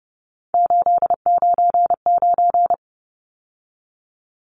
Morse code, heard as a keyed tone, about 30 wpm, 700 Hz. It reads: 899
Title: Numbers - 3-Digit 30wpm